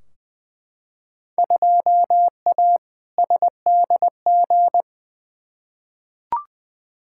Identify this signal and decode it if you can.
Morse code, heard as a keyed tone, about 20 wpm, 700 Hz. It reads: 2A SDG
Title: Field Day Exchanges Encore - 20wpm